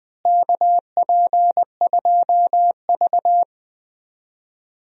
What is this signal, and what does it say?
Morse code, heard as a keyed tone, about 20 wpm, 700 Hz. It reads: KP2V
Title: DXCC-100 Call Signs - 4 Characters - Repeated 3x - 20wpm